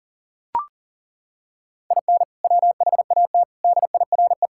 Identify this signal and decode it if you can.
Morse code, heard as a keyed tone, about 40 wpm, 700 Hz. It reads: IN WHAT DIRE
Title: Sentences from Top 1000 Words - Review Part 5 - 40wpm